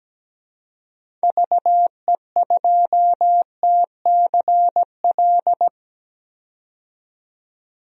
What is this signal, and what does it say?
Morse code, heard as a keyed tone, about 17 wpm, 700 Hz. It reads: VE2TCL
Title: DXCC-100 Call Signs - 6 Characters - 17wpm